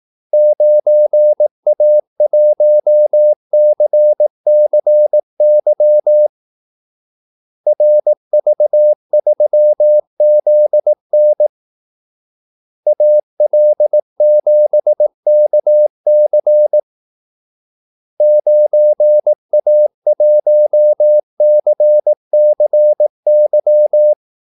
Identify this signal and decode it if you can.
Morse code, heard as a keyed tone, about 18 wpm, 600 Hz. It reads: 9A1CCY RV3ZN AL7KC 9A1CCY
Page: Kaldesignaler 18 wpm